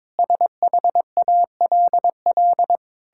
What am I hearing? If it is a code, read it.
Morse code, heard as a keyed tone, about 21 wpm, 700 Hz. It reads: SHALL
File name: Sentences with 9-Word Phrases - 22wpm